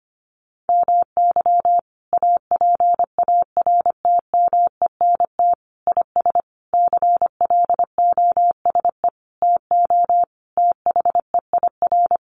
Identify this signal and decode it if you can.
Morse code, heard as a keyed tone, about 25 wpm, 700 Hz. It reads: MY APARTMENT IS CLOSE TO THEIR
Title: Sentences from Top 900 Words 25wpm